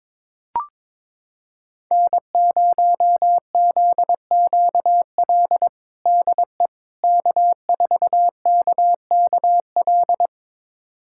Morse code, heard as a keyed tone, about 21 wpm, 700 Hz. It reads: N0ZQL DE K4KKL
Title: USA Callsign DE Callsign - 5 Characters - 22wpm